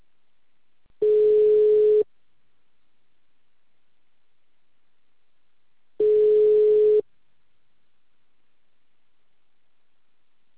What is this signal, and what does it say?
Ringback tone (Europe, 425 Hz).
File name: out- -2044- - - .27082